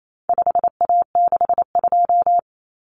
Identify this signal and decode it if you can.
Morse code, heard as a keyed tone, about 28 wpm, 700 Hz. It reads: 5A62
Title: Common Cut Numbers - 4-Digit 28wpm